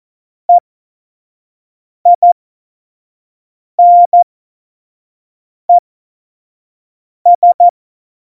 Morse code, heard as a keyed tone, about 14 wpm, spaced as for 5 wpm, 700 Hz. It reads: EINES
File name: Morsekurs / Witze / Witz 08. Mai 2018, 25 F70